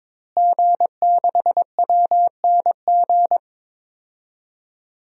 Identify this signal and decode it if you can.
Morse code, heard as a keyed tone, about 21 wpm, 700 Hz. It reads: G6WNG
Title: DXCC-100 Call Signs - 5 Characters - Repeated 3x - 22wpm